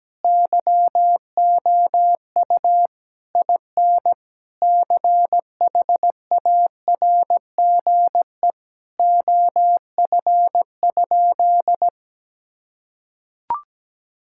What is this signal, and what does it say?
Morse code, heard as a keyed tone, about 17 wpm, 700 Hz. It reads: YOU IN CHARGE OF?
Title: Sentences from Top 1000 Words - Review Part 6 - 17wpm